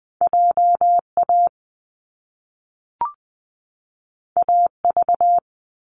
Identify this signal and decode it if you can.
Morse code, heard as a keyed tone, about 20 wpm, 700 Hz. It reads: J A A V
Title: Top Two Letter Combinations in Words 20wpm